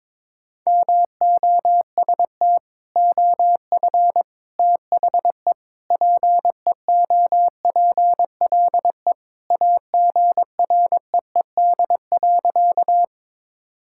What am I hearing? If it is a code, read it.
Morse code, heard as a keyed tone, about 22 wpm, 700 Hz. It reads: MOST OF THE PEOPLE AGREED.
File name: practice / 22wpm / Sentences with 3-Word Phrases - 22wpm